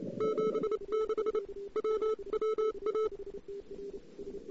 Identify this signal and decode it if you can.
Morse code, heard as a keyed tone, about 28 wpm, 1300 Hz. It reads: 76 WWA
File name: CW / 2008-ss-cw-k5tr-op-n5rz -00:00